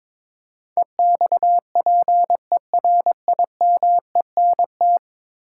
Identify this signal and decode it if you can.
Morse code, heard as a keyed tone, about 22 wpm, 700 Hz. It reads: EXPERIMENT
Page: morse-code-ninja-practice-22wpm